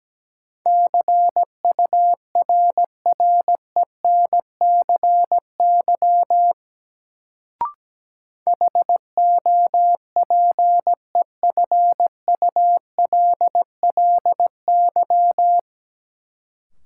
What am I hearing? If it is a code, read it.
Morse code, heard as a keyed tone, about 17 wpm, 700 Hz. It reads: CURRENCY HOPEFULLY
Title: Sentences from Top 1500 Words - New Word Drill - 17wpm